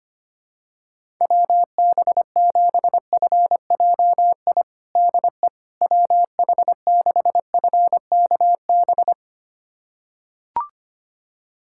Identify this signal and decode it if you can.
Morse code, heard as a keyed tone, about 25 wpm, 700 Hz. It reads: WB7FJI DE WH6FKB
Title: USA Callsign DE Callsign - 6 Characters - 25wpm